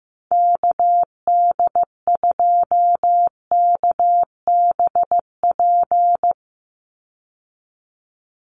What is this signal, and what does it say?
Morse code, heard as a keyed tone, about 15 wpm, 700 Hz. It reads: KD2KBP
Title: USA Call Signs 15wpm